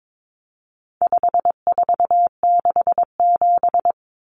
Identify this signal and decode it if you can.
Morse code, heard as a keyed tone, about 22 wpm, 700 Hz. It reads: 5467
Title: Common Cut Numbers - 4-Digit 22wpm